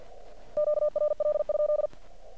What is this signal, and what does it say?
Morse code, heard as a keyed tone, about 48 wpm, 1200 Hz. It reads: QRL?
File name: archived / CRSnet